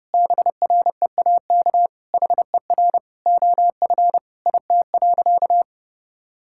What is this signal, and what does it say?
Morse code, heard as a keyed tone, about 30 wpm, 700 Hz. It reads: BREAK HER OF IT.
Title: Sentences from Top 1000 Words - Review Part 6 - 30wpm